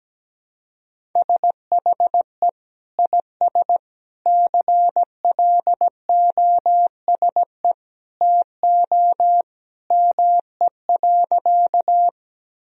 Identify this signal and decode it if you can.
Morse code, heard as a keyed tone, about 17 wpm, 700 Hz. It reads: SHE IS CLOSE TO ME.
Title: Sentences from Top 1000 Words - Review Part 1 - 17wpm